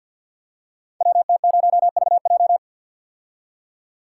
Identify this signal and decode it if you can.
Morse code, heard as a keyed tone, about 48 wpm, 700 Hz. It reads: WT0VJ
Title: Code Groups - 5 Letter:Numbers - 50wpm